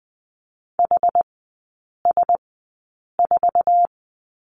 Morse code, heard as a keyed tone, about 20 wpm, spaced as for 8 wpm, 700 Hz. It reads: HS4
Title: 100 Lesson 17e - Letter u - Three Characters 20wpm